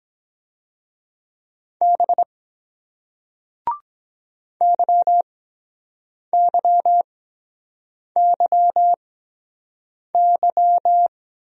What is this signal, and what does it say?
Morse code, heard as a keyed tone, about 23 wpm, spaced as for 7 wpm, 700 Hz. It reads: B YYYY